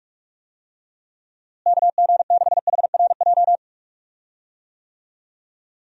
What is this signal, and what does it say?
Morse code, heard as a keyed tone, about 41 wpm, 700 Hz. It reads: KG6HRJ
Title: USA Call Signs - 6 Characters - Repeated 3x - 45wpm